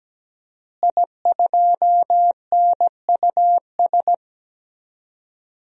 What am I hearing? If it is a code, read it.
Morse code, heard as a keyed tone, about 17 wpm, 700 Hz. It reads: I2NUS
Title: DXCC-100 Call Signs - 5 Characters - Repeated 2x - 17wpm